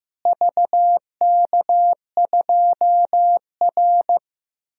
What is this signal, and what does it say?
Morse code, heard as a keyed tone, about 15 wpm, 700 Hz. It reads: VK2R
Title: DXCC-100 Call Signs - 4 Characters - Repeated 3x - 15wpm